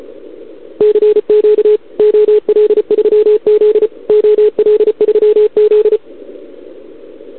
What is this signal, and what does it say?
Morse code, heard as a keyed tone, about 34 wpm, 440 Hz. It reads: CQ OL3Z OL3Z
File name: CQWWCW2021 / OL3Z